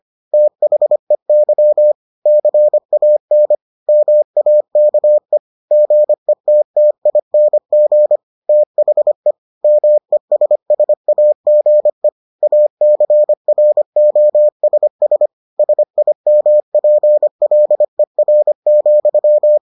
Morse code, heard as a keyed tone, about 25 wpm, 600 Hz. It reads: THEY CAN MAKE GETTING THE MESSAGE ACROSS SIMPLER,